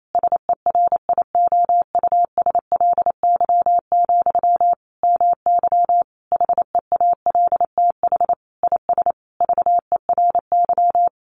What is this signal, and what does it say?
Morse code, heard as a keyed tone, about 28 wpm, 700 Hz. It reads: SERIOUSLY, MY HEALTH IS VERY
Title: Sentences from Top 1200 Words 28wpm